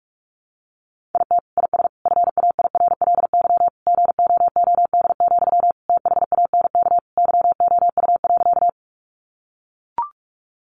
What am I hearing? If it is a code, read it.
Morse code, heard as a keyed tone, about 44 wpm, 700 Hz. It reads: IT IS FAIRLY GOOD, THANK YOU.
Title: Sentences from Top 1100 Words 45wpm